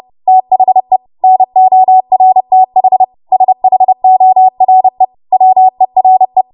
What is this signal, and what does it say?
Morse code, heard as a keyed tone, about 30 wpm, 750 Hz. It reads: THE NORTH SHORE WERE
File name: Morse / 30WPM